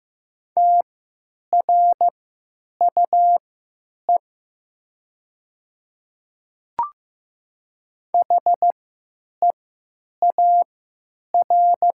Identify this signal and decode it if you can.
Morse code, heard as a keyed tone, about 15 wpm, spaced as for 9 wpm, 700 Hz. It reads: TRUE HEAR
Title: 3x Character Spacing - Top 500 Words - 15wpm